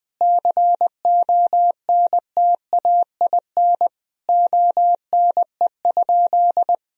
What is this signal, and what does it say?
Morse code, heard as a keyed tone, about 20 wpm, 700 Hz. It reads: CONTAIN ONE?
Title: Sentences from Top 1000 Words - Review Part 2 - 20wpm